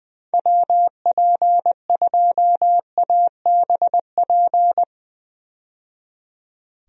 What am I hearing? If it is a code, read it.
Morse code, heard as a keyed tone, about 21 wpm, 700 Hz. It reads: WP2ABP
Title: USA Call Signs - 6 Characters - 20wpm